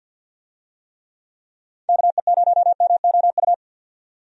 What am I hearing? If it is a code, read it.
Morse code, heard as a keyed tone, about 51 wpm, 700 Hz. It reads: KE0MOU
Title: USA Call Signs - 6 Characters - 50wpm